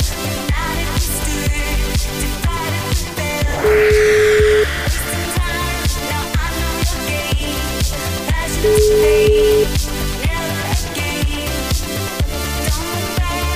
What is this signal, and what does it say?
Ringback tone (Europe, 425 Hz), with noise.